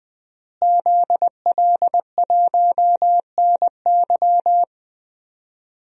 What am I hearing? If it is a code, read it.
Morse code, heard as a keyed tone, about 20 wpm, 700 Hz. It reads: ZL1NY
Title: DXCC-100 Call Signs - 5 Characters - Repeated 3x - 20wpm